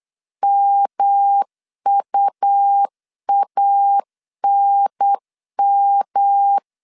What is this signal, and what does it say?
Morse code, heard as a keyed tone, about 8 wpm, 800 Hz. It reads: MUANM
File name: CW / DET / 1 3 1500C S B 8wpm 4141